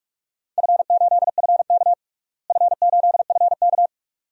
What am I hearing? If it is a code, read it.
Morse code, heard as a keyed tone, about 46 wpm, 700 Hz. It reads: F8FK F8FK
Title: DXCC-100 Call Signs - 4 Characters - Repeated 2x - 45wpm